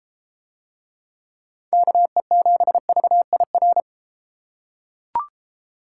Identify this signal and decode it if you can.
Morse code, heard as a keyed tone, about 33 wpm, 700 Hz. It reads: KE7VIR